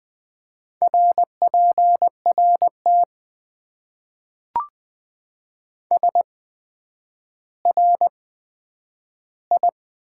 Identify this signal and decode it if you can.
Morse code, heard as a keyed tone, about 20 wpm, spaced as for 10 wpm, 700 Hz. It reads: RPRT SRI
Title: Top 90 QSO Elements - 8x Character Spacing - 20wpm